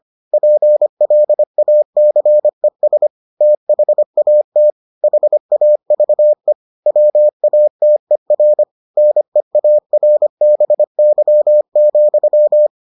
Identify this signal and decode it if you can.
Morse code, heard as a keyed tone, about 25 wpm, 600 Hz. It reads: PLACES THAT HAVE WATER NEARBY,